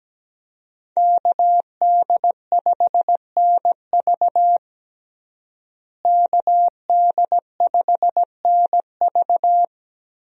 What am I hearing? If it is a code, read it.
Morse code, heard as a keyed tone, about 17 wpm, 700 Hz. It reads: KD5NV KD5NV
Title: USA Call Signs - 5 Characters - Repeated 3x - 17wpm